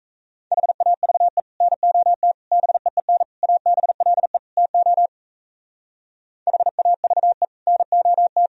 Morse code, heard as a keyed tone, about 34 wpm, 700 Hz. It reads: HAVE NOT BEEN ABLE TO HAVE NOT
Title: Speed-Racing - Sets of 5 Words 28wpm